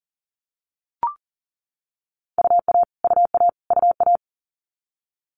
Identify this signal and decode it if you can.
Morse code, heard as a keyed tone, about 40 wpm, 700 Hz. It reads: UA UA UA
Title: Multi-letter Phonemes Repeated 3x 40wpm